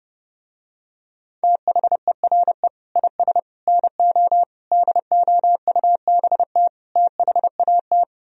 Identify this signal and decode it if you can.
Morse code, heard as a keyed tone, about 30 wpm, 700 Hz. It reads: THERE IS NO DOUBT THAT